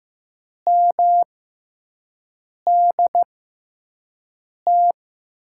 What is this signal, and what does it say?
Morse code, heard as a keyed tone, about 15 wpm, spaced as for 5 wpm, 700 Hz. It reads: MDT
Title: USA Call Signs - 5 Characters - 6x Character Spacing - 15wpm